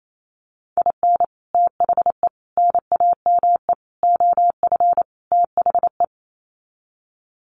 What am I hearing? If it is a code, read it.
Morse code, heard as a keyed tone, about 28 wpm, 700 Hz. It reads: IN THE NAME OF THE
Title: Sets of 5 Words Encore 28wpm